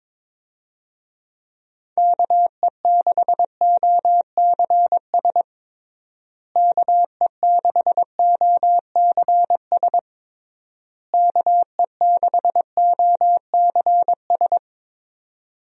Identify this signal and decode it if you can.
Morse code, heard as a keyed tone, about 22 wpm, 700 Hz. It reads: KE6OCS KE6OCS KE6OCS
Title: USA Call Signs - 6 Characters - Repeated 3x - 22wpm